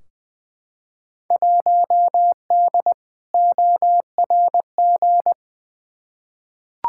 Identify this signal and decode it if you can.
Morse code, heard as a keyed tone, about 20 wpm, 700 Hz. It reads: 1D ORG
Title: Field Day Exchanges - 20wpm